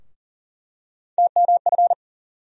Morse code, heard as a keyed tone, about 39 wpm, 700 Hz. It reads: TMF
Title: USA Call Sign Postfixes - 3 Letters 40wpm